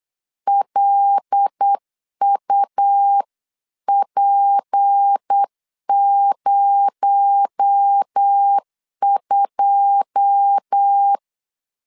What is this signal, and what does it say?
Morse code, heard as a keyed tone, about 8 wpm, 800 Hz. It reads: LUP02
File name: CW / DET / 1 9 1500C S A 8wpm 5637